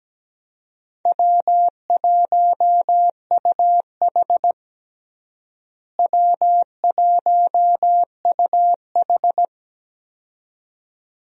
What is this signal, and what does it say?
Morse code, heard as a keyed tone, about 17 wpm, 700 Hz. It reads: W1UH W1UH
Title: USA Call Signs - 4 Characters - Repeated 2x - 17wpm